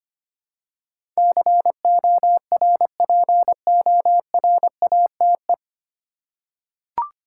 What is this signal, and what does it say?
Morse code, heard as a keyed tone, about 25 wpm, 700 Hz. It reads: CORPORATE
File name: Nine Letter Words - 25wpm